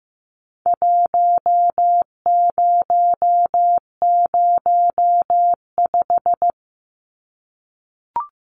Morse code, heard as a keyed tone, about 15 wpm, 700 Hz. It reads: 1005
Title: Numbers - Contest Style - 4-Digit 15wpm